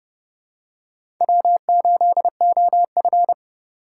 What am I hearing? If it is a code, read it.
Morse code, heard as a keyed tone, about 30 wpm, 700 Hz. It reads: W8OF
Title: USA Call Signs - 2x Character Spacing - 30wpm